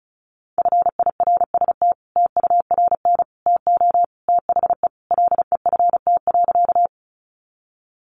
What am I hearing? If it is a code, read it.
Morse code, heard as a keyed tone, about 34 wpm, 700 Hz. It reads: FIRST TURN TO THE LEFT.
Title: Sentences from Top 300 Words 35wpm